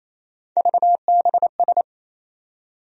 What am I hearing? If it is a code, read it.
Morse code, heard as a keyed tone, about 27 wpm, 700 Hz. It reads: VBS
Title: USA Call Sign Postfixes - 3 Letters 28wpm